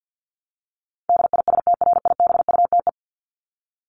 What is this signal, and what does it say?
Morse code, heard as a keyed tone, about 50 wpm, 700 Hz. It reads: DISTRIBUTE
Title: Top 1501-1750 Words 50wpm